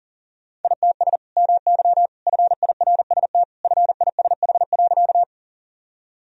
Morse code, heard as a keyed tone, about 40 wpm, 700 Hz. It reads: ITS MY FIRST FISH.
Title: Sentences from Top 1000 Words - Review Part 1 - 40wpm